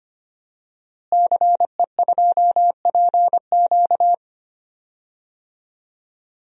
Morse code, heard as a keyed tone, about 25 wpm, 700 Hz. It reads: CE2PQ
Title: DXCC-100 Call Signs - 5 Characters - Repeated 2x - 25wpm